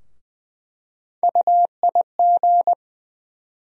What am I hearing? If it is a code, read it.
Morse code, heard as a keyed tone, about 20 wpm, 700 Hz. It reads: UIG
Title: USA Call Sign Postfixes - 3 Letters 20wpm